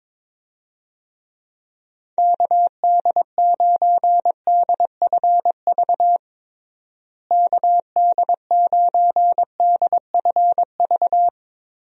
Morse code, heard as a keyed tone, about 22 wpm, 700 Hz. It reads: KD9DFV KD9DFV
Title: USA Call Signs - 6 Characters - Repeated 3x - 22wpm